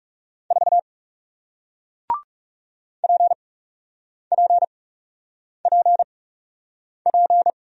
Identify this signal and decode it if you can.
Morse code, heard as a keyed tone, about 39 wpm, spaced as for 8 wpm, 700 Hz. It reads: 4 PPPP